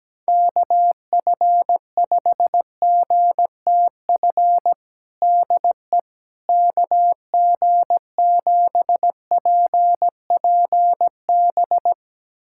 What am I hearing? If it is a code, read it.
Morse code, heard as a keyed tone, about 17 wpm, 700 Hz. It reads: KF5GTF DE KG7PPB
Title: USA Callsign DE Callsign 17wpm